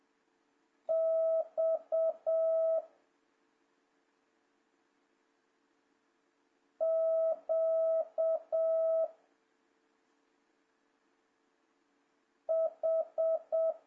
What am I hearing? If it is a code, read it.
Morse code, heard as a keyed tone, about 7 wpm, 650 Hz. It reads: X Q H
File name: jh8cbh / 2026cw